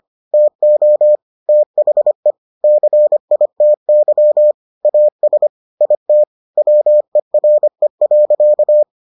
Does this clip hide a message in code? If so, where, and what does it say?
Yes: Morse code from 0.3 to 8.8 s, heard as a keyed tone, about 25 wpm, 600 Hz. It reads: TO THE CITY AS IT WERE.